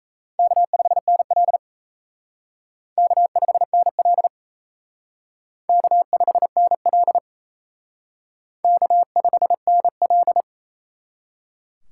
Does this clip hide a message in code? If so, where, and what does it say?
Yes: Morse code from 0.4 to 10.4 s, heard as a keyed tone, about 36 wpm, 700 Hz. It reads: K5NL K5NL K5NL K5NL